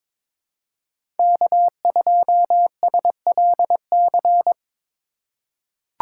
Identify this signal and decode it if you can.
Morse code, heard as a keyed tone, about 22 wpm, 700 Hz. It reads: K2SLC
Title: USA Call Signs - 5 Characters - 2x Character Spacing - 22wpm